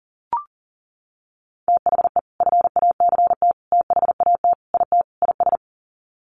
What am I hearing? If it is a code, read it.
Morse code, heard as a keyed tone, about 40 wpm, 700 Hz. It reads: THE FACT THAT IT IS
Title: Sets of 5 Words Encore2 40wpm